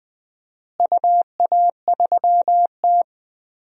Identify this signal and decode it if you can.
Morse code, heard as a keyed tone, about 20 wpm, 700 Hz. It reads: UA3T
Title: DXCC-100 Call Signs - 4 Characters - Repeated 3x - 20wpm